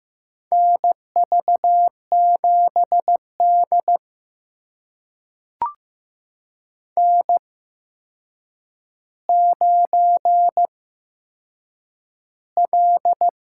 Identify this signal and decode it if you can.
Morse code, heard as a keyed tone, about 15 wpm, spaced as for 7 wpm, 700 Hz. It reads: NV7D N9L
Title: USA Call Signs - 4 Characters - 8x Character Spacing - 15wpm